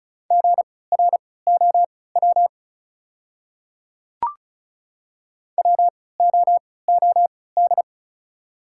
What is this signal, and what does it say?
Morse code, heard as a keyed tone, about 34 wpm, spaced as for 20 wpm, 700 Hz. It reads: GROW WOOD
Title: 3x Character Spacing - Top 300 Words - 35wpm